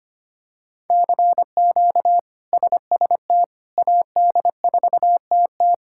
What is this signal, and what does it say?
Morse code, heard as a keyed tone, about 25 wpm, 700 Hz. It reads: CQ SST AD4TT
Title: K1USN SST QSO Practice - 25wpm